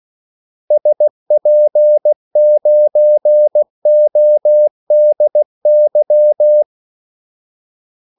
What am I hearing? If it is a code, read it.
Morse code, heard as a keyed tone, about 16 wpm, 600 Hz. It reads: SP9ODY
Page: Kaldesignaler 16 wpm